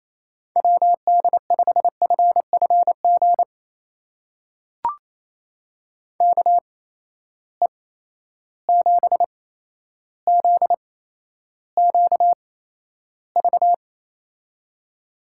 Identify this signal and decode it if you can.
Morse code, heard as a keyed tone, about 28 wpm, spaced as for 12 wpm, 700 Hz. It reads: WD5FFG KE7ZQV
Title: USA Call Signs - 6 Characters - 8x Character Spacing - 28wpm